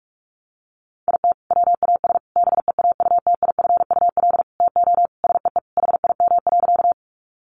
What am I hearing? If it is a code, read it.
Morse code, heard as a keyed tone, about 44 wpm, 700 Hz. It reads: IT WAS BEAUTIFUL TO SEE HIM.
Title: Sentences from Top 900 Words 45wpm